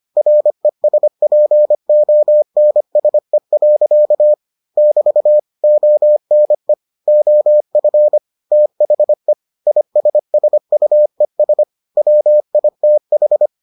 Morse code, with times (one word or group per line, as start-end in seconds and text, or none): 0.2-4.3 RESPONSE.
4.8-5.4 <BT>
5.6-6.7 ONE
7.1-8.2 OF
8.5-9.3 THE
9.7-11.6 ISSUES
12.0-13.5 WITH